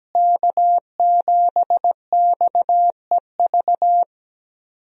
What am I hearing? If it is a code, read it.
Morse code, heard as a keyed tone, about 17 wpm, 700 Hz. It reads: K7XEV
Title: USA Call Signs - 5 Characters - Repeated 2x - 17wpm